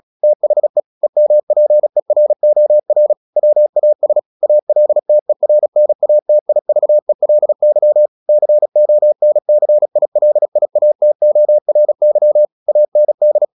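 Morse code, heard as a keyed tone, about 36 wpm, 600 Hz. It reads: THE EMPEROR WAS ALTERNATIVELY CONCILIATORY AND
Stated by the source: Morse practice at 36 wpm, text of Ken Follett's novel Never, from CW med Gnister.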